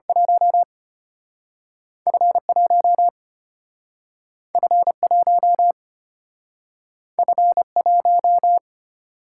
Morse code, heard as a keyed tone, about 34 wpm, spaced as for 6 wpm, 700 Hz. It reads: 1F 1F 1F 1